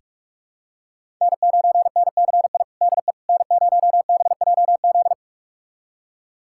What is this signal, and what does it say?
Morse code, heard as a keyed tone, about 43 wpm, 700 Hz. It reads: N9NCI DE N0BJZ
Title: USA Callsign DE Callsign - 5 Characters - 45wpm